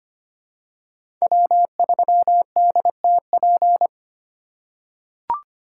Morse code, heard as a keyed tone, about 25 wpm, 700 Hz. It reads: W3DTP
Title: USA Call Signs - 5 Characters - 8x Character Spacing - 25wpm